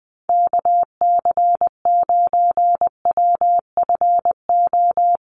Morse code, heard as a keyed tone, about 20 wpm, 700 Hz. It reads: KC9WFO
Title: USA Call Signs Repeated 3x 20wpm